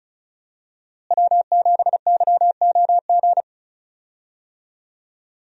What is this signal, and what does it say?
Morse code, heard as a keyed tone, about 36 wpm, 700 Hz. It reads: W7YOG
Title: USA Call Signs - 5 Characters - Repeated 3x - 35wpm